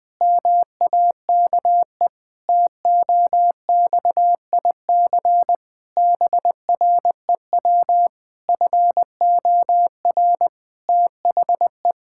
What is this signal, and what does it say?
Morse code, heard as a keyed tone, about 20 wpm, 700 Hz. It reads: MAKE TOXIC BREW FOR THE
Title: Pangram Challenge v2 - 20wpm